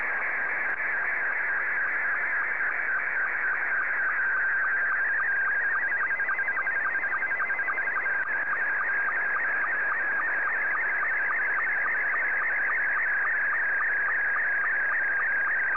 SSTV (slow-scan television), Scottie 2.
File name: sstv / S2-VK3DEK